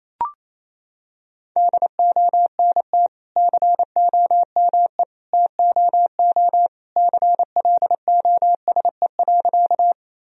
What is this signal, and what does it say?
Morse code, heard as a keyed tone, about 28 wpm, 700 Hz. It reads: DONT COME TOO CLOSE.
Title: Sentences from Top 1000 Words - Review Part 1 - 28wpm